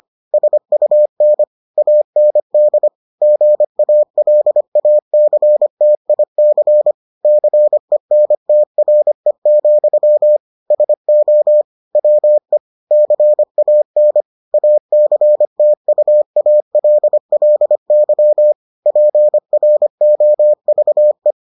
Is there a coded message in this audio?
Yes: Morse code from 0.3 to 21.3 s, heard as a keyed tone, about 25 wpm, 600 Hz. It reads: SUN AND GALACTIC CENTRE, SO WE CAN ACTUALLY PROVE